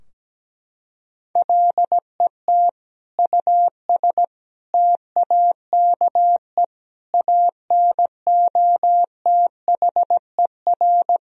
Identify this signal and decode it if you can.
Morse code, heard as a keyed tone, about 17 wpm, 700 Hz. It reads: LET US TAKE ANOTHER
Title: Sentences from Top 1000 Words - Review Part 2 - 17wpm